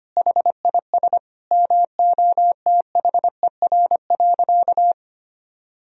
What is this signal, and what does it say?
Morse code, heard as a keyed tone, about 25 wpm, 700 Hz. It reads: HIS MOTHER.
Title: Sentences from Top 1000 Words - Review Part 1 - 25wpm